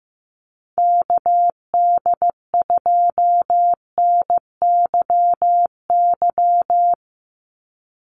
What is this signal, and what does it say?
Morse code, heard as a keyed tone, about 15 wpm, 700 Hz. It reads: KD2NYY
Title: USA Call Signs 15wpm